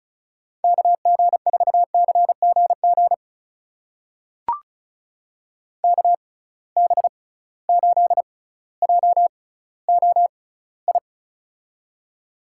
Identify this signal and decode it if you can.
Morse code, heard as a keyed tone, about 36 wpm, spaced as for 18 wpm, 700 Hz. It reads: KG4CGG KB8JOI
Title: USA Call Signs - 6x Character Spacing - 35wpm